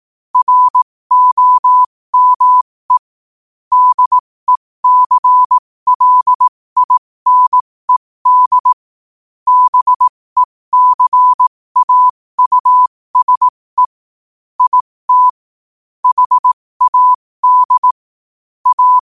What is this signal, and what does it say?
Morse code, heard as a keyed tone, about 18 wpm, spaced as for 15 wpm, 1000 Hz. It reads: ROME DECLINED BECAUSE IT HAD A
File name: qotdfiles / 15wpm